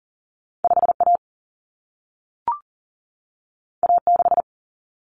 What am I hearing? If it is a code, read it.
Morse code, heard as a keyed tone, about 40 wpm, 700 Hz. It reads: 5A A6
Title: Common Cut Numbers - 2-Digit 40wpm